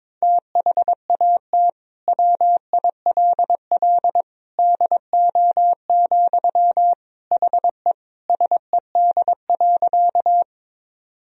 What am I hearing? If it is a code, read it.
Morse code, heard as a keyed tone, about 22 wpm, 700 Hz. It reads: THAT WILL DO, HE SED.
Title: Sentences from Top 100 Words and Abbreviations 22wpm